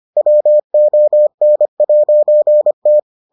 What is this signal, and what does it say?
Morse code, heard as a keyed tone, about 25 wpm, 600 Hz. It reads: WON'T